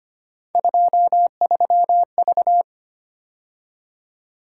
Morse code, heard as a keyed tone, about 25 wpm, 700 Hz. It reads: 23V
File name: Three Letter-Number Combinations - 25wpm